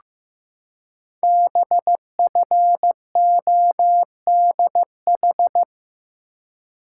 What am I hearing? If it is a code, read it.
Morse code, heard as a keyed tone, about 15 wpm, 700 Hz. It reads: BFODH
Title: Code Groups - 5 Letters - 15wpm